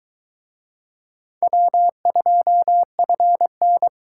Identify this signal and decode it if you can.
Morse code, heard as a keyed tone, about 22 wpm, 700 Hz. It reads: W2FN